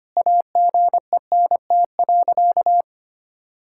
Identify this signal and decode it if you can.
Morse code, heard as a keyed tone, about 25 wpm, 700 Hz. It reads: AGENT.
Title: Sentences from Top 1600 Words 25wpm